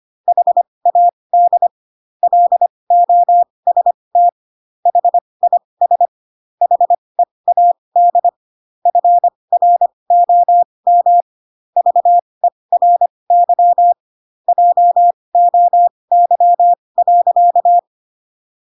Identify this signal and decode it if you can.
Morse code, heard as a keyed tone, about 25 wpm, spaced as for 20 wpm, 700 Hz. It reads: HAD LOST HIS HEAD FROM VERY JOY.
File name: Pinocchio / Pinocchio ch8 15 25 ik1hge